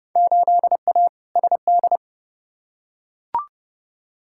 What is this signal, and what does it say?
Morse code, heard as a keyed tone, about 30 wpm, 700 Hz. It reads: 8A SD
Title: Field Day Exchanges Encore - 30wpm